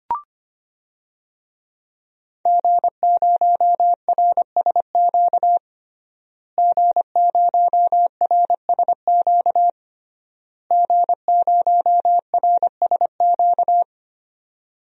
Morse code, heard as a keyed tone, about 25 wpm, 700 Hz. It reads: G0RSQ G0RSQ G0RSQ
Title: DXCC-100 Call Signs - 5 Characters - Repeated 3x - 25wpm